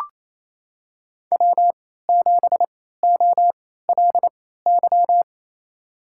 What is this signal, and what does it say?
Morse code, heard as a keyed tone, about 28 wpm, spaced as for 16 wpm, 700 Hz. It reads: W7OLY
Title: USA Call Signs - 3x Character Spacing - 28wpm